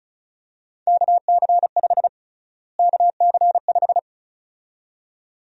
Morse code, heard as a keyed tone, about 35 wpm, 700 Hz. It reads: KC5 KC5
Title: USA Call Sign Prefixes - Repeated 2x - 35wpm